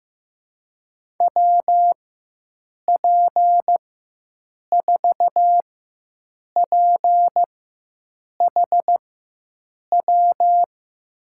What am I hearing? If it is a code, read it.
Morse code, heard as a keyed tone, about 15 wpm, spaced as for 7 wpm, 700 Hz. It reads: WP4PHW
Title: USA Call Signs - 4x Character Spacing - 15wpm